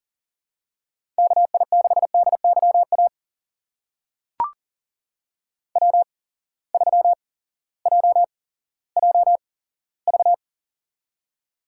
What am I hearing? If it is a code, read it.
Morse code, heard as a keyed tone, about 40 wpm, spaced as for 18 wpm, 690 Hz. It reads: KI6DYA W3JJV
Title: USA Call Signs - 8x Character Spacing - 40wpm